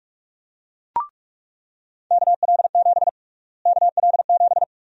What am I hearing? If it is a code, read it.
Morse code, heard as a keyed tone, about 46 wpm, 700 Hz. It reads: KL7 KL7
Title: USA Call Sign Prefixes - Repeated 2x - 45wpm